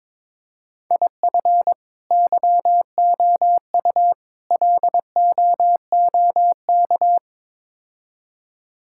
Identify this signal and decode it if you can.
Morse code, heard as a keyed tone, about 22 wpm, 700 Hz. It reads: IF YOU LOOK
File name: practice / 22wpm / Build A Phrase - 5-Word Phrases - 22wpm